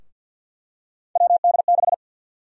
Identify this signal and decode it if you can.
Morse code, heard as a keyed tone, about 52 wpm, 700 Hz. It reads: WD6
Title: USA Call Sign Prefixes - Repeated 2x - 50wpm